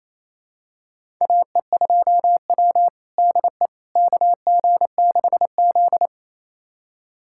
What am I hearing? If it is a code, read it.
Morse code, heard as a keyed tone, about 28 wpm, 700 Hz. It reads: AE2W DE KG6Z
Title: USA Callsign DE Callsign - 4 Characters - 28wpm